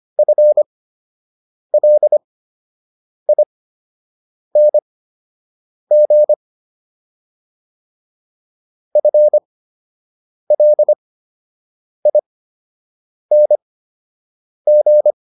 Morse code, heard as a keyed tone, about 25 wpm, spaced as for 7 wpm, 600 Hz. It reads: FLING FLING